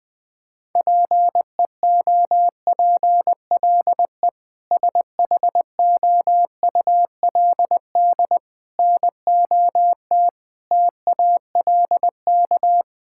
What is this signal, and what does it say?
Morse code, heard as a keyed tone, about 19 wpm, 700 Hz. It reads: PEOPLE SHOULD NOT TALK
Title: Sentences from Top 1000 Words - Review Part 6 - 20wpm